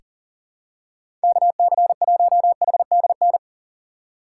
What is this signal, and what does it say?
Morse code, heard as a keyed tone, about 40 wpm, 700 Hz. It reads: KC1HDN
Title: USA Call Signs - 2x Character Spacing - 40wpm